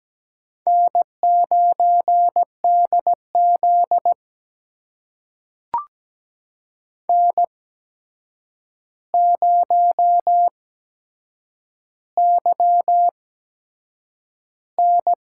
Morse code, heard as a keyed tone, about 17 wpm, spaced as for 7 wpm, 700 Hz. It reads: N9DZ N0YN
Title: USA Call Signs - 4 Characters - 8x Character Spacing - 17wpm